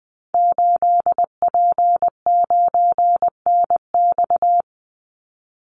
Morse code, heard as a keyed tone, about 20 wpm, 700 Hz. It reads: 8P9NX
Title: Call Signs Repeated 2x - 20wpm